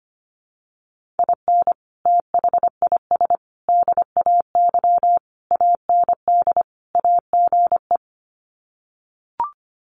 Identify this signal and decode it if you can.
Morse code, heard as a keyed tone, about 25 wpm, 700 Hz. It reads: IN THIS DAY AND AGE
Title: Sets of 5 Words Encore 25wpm